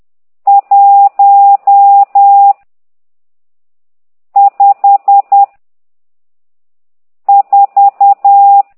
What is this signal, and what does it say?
Morse code, heard as a keyed tone, about 10 wpm, 800 Hz. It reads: 1 5 4
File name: ct2idw / exerc12